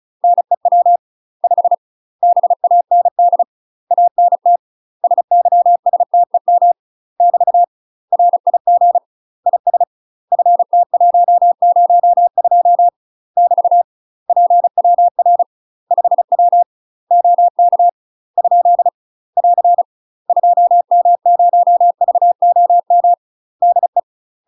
Morse code, heard as a keyed tone, about 34 wpm, 700 Hz. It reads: NEW 5 BAND ANT SYSTEM <BT> RIG IS FT102 <BT> PWR 5W OK ? <AR> 2M0VOM DE
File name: UK MASTER QSOs TRAINING / 35WPM / UK MASTER QSOs 0008